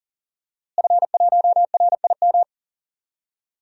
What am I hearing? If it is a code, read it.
Morse code, heard as a keyed tone, about 39 wpm, 700 Hz. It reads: F1RIM
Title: DXCC-100 Call Signs - 5 Characters - Repeated 2x - 40wpm